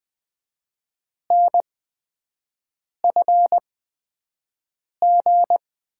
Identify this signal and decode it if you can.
Morse code, heard as a keyed tone, about 20 wpm, spaced as for 5 wpm, 700 Hz. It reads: NFG
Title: USA Call Signs - 5 Characters - 8x Character Spacing - 20wpm